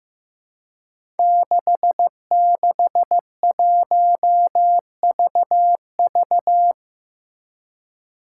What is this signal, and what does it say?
Morse code, heard as a keyed tone, about 15 wpm, 700 Hz. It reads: 661VV
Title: Five Challenging Letter-Number Combinations 15wpm